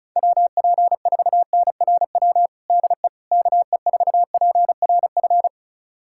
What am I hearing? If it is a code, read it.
Morse code, heard as a keyed tone, about 34 wpm, 700 Hz. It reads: WP4NRW DE KE4PRF
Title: USA Callsign DE Callsign - 6 Characters - 35wpm